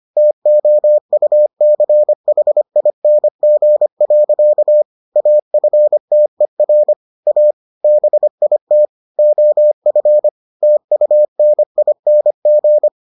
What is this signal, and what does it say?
Morse code, heard as a keyed tone, about 25 wpm, 600 Hz. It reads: TOUCHING. AFTER A BIT OF TUNING